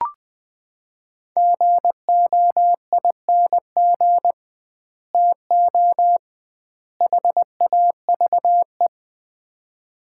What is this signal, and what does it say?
Morse code, heard as a keyed tone, about 20 wpm, 700 Hz. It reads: GOING TO HAVE
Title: 2x Word-Spacing - Sets of 3 Words - 20wpm